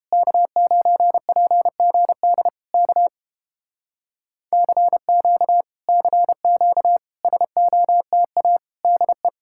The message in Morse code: K9PGD K CQ CQ SOTA DE

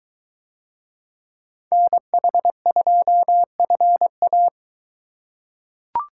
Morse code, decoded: NH2FA

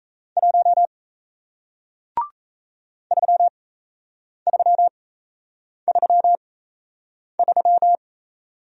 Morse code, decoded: 1 3333

